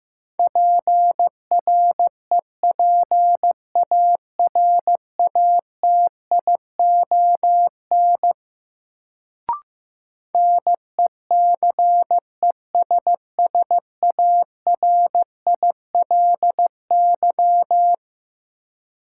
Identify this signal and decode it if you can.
Morse code, heard as a keyed tone, about 15 wpm, 700 Hz. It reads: PREPARATION NECESSARILY
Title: Sentences from Top 1400 Words - New Word Drill - 15wpm